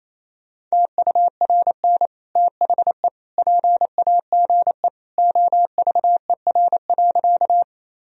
Morse code, heard as a keyed tone, about 28 wpm, 700 Hz. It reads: TURN THE PAGE OVER.
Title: Sentences from Top 1000 Words - Review Part 1 - 28wpm